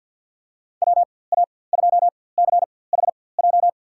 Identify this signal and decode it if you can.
Morse code, heard as a keyed tone, about 53 wpm, spaced as for 24 wpm, 700 Hz. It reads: WA2CHJ
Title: USA Call Signs - 4x Character Spacing - 50wpm